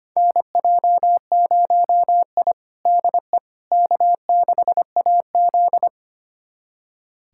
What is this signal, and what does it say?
Morse code, heard as a keyed tone, about 25 wpm, 700 Hz. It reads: NJ0I DE K6AZ
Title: USA Callsign DE Callsign - 4 Characters - 25wpm